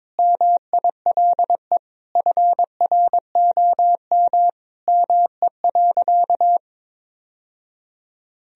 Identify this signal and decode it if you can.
Morse code, heard as a keyed tone, about 22 wpm, 700 Hz. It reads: MILE FROM ME.
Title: Sentences from Top 1000 Words - Review Part 1 - 22wpm